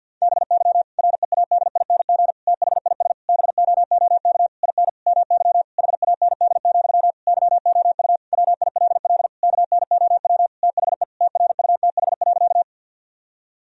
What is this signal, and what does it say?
Morse code, heard as a keyed tone, about 50 wpm, 700 Hz. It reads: BY READING THIS BOOK IN MY HAND, YOU WILL KNOW THE TRUTH.
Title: Sentences from Top 1000 Words - Review Part 3 - 50wpm